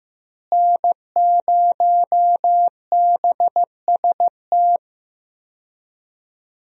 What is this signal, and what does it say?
Morse code, heard as a keyed tone, about 15 wpm, 700 Hz. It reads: N0BST
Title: USA Call Signs - 5 Characters - Repeated 2x - 15wpm